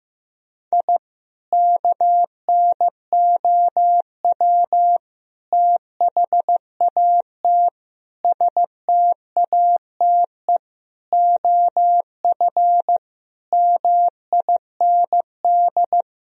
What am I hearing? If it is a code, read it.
Morse code, heard as a keyed tone, about 15 wpm, 700 Hz. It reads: I KNOW THAT STATE OF MIND
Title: Sentences from Top 1000 Words - Review Part 2 - 15wpm